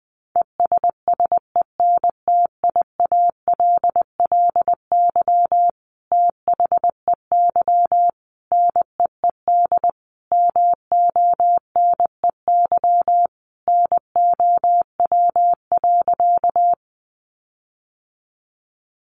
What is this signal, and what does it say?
Morse code, heard as a keyed tone, about 20 wpm, 700 Hz. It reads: ESSENTIALLY THEY NEED MONEY NOW.
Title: Sentences from Top 1300 Words 20wpm